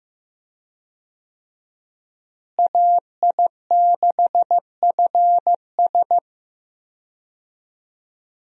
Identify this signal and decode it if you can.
Morse code, heard as a keyed tone, about 15 wpm, 700 Hz. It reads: AI6FS
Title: USA Call Signs - 5 Characters - Repeated 3x - 15wpm